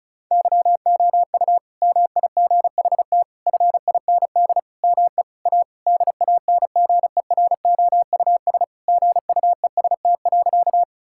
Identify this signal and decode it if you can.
Morse code, heard as a keyed tone, about 36 wpm, 700 Hz. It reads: YOU MIGHT FIND ME A DANGEROUS GUEST.
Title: Sentences from Top 1600 Words 35wpm